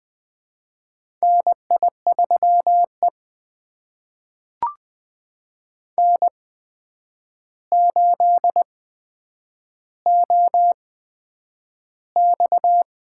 Morse code, heard as a keyed tone, about 20 wpm, spaced as for 9 wpm, 700 Hz. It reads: NI3E N8OX
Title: USA Call Signs - 4 Characters - 8x Character Spacing - 20wpm